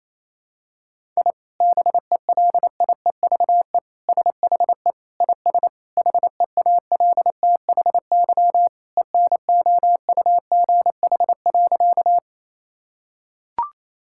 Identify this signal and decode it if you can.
Morse code, heard as a keyed tone, about 28 wpm, 700 Hz. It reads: I BELIEVE SHE IS HEALTHY ENOUGH.
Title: Sentences from Top 1000 Words - Review Part 5 - 28wpm